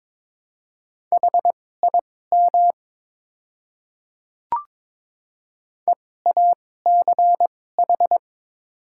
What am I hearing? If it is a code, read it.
Morse code, heard as a keyed tone, about 22 wpm, spaced as for 16 wpm, 700 Hz. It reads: HIM EACH
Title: 2x Character Spacing - Top 100 Words - 22wpm